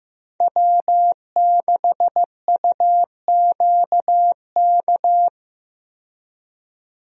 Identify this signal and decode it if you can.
Morse code, heard as a keyed tone, about 15 wpm, 700 Hz. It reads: W6UQK
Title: USA Call Signs - 5 Characters - 15wpm